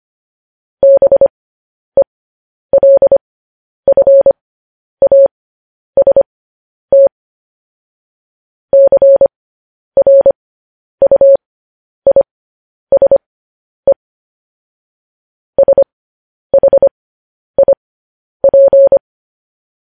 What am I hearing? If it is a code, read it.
Morse code, heard as a keyed tone, about 25 wpm, spaced as for 10 wpm, 550 Hz. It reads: BELFAST CRUISE SHIP
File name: podcast / News headlines-1x-25x10